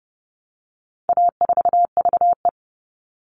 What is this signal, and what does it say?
Morse code, heard as a keyed tone, about 30 wpm, 700 Hz. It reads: A4VE
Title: All Cut Numbers - 4-Digit 30wpm